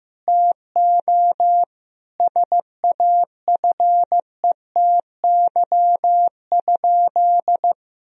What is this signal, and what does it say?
Morse code, heard as a keyed tone, about 15 wpm, 700 Hz. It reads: TO SAFETY?
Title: Sentences from Top 1000 Words - Review Part 4 - 15wpm